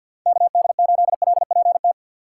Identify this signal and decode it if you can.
Morse code, heard as a keyed tone, about 52 wpm, 700 Hz. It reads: KD8LPT
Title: USA Call Signs - 6 Characters - 6x Character Spacing - 50wpm